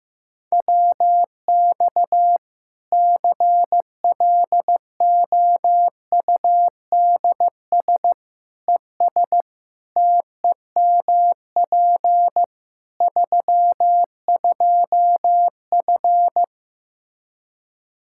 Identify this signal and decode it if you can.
Morse code, heard as a keyed tone, about 15 wpm, 700 Hz. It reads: WX CLOUDS ES TEMP 32F
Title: Practice QSO - Exchange 2 - Info - 15wpm